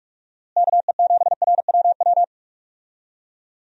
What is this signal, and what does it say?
Morse code, heard as a keyed tone, about 46 wpm, 700 Hz. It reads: KE7RWW
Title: USA Call Signs - 6 Characters - Repeated 2x - 45wpm